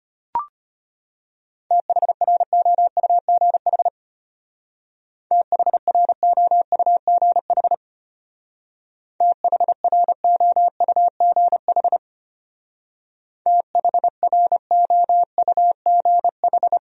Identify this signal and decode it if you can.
Morse code, heard as a keyed tone, about 31 wpm, 700 Hz. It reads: THROUGH THROUGH THROUGH THROUGH
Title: Speed-Racing Top 500 Words 25wpm